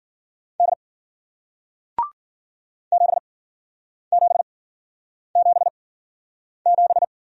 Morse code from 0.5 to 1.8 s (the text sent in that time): D